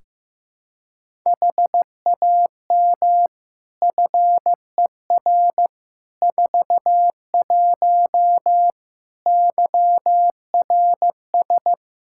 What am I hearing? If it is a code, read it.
Morse code, heard as a keyed tone, about 15 wpm, 700 Hz. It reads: HAM FER 41 YRS
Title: Practice QSO - Exchange 3 - More Info - 15wpm